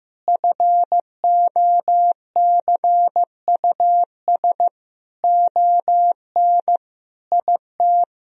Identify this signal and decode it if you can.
Morse code, heard as a keyed tone, about 14 wpm, 700 Hz. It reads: FOCUS ON IT
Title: Sentences from Top 1000 Words - Review Part 4 - 15wpm